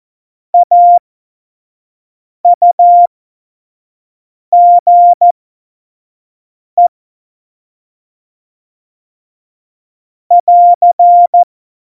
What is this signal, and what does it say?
Morse code, heard as a keyed tone, about 14 wpm, spaced as for 5 wpm, 700 Hz. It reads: AUGE <AR>
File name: Morsekurs / Witze / Witz 15. Mai 2018, 25 F70